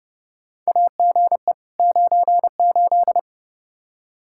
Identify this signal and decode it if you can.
Morse code, heard as a keyed tone, about 30 wpm, 700 Hz. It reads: AGE 98
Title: Practice QSO - Exchange 3 - More Info - 30wpm